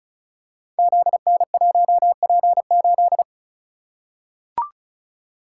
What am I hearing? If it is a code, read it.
Morse code, heard as a keyed tone, about 35 wpm, 700 Hz. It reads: ZN1P8
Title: Code Groups - 5 Letter:Numbers - 35wpm